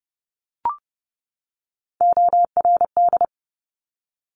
Morse code, heard as a keyed tone, about 30 wpm, 690 Hz. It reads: ORD
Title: Top Three Letter Combinations in Words 30wpm